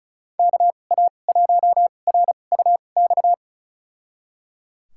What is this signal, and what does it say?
Morse code, heard as a keyed tone, about 36 wpm, spaced as for 25 wpm, 700 Hz. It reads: KA1RUX
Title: USA Call Signs - 6 Characters - 2x Character Spacing - 35wpm